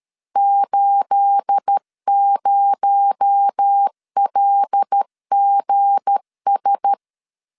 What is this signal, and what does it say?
Morse code, heard as a keyed tone, about 13 wpm, 800 Hz. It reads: 80LGS
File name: LEZIONI / 10 SIGNS / 10A SLASH 12wpm 2314